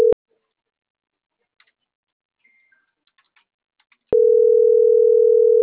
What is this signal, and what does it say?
Ringback tone (North America), 440 + 480 Hz.